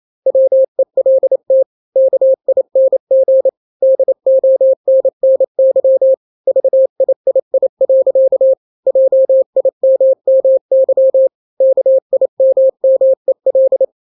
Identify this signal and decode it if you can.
Morse code, heard as a keyed tone, about 28 wpm, 530 Hz. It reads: WELT KING DONNY VIII. JIMMY KIMMEL
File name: daily / 27wpm